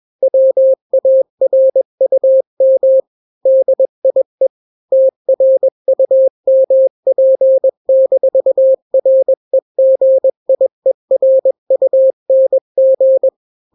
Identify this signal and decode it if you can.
Morse code, heard as a keyed tone, about 21 wpm, 530 Hz. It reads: WARUM DIE TRUMP-REGIERUNG
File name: daily / 20wpm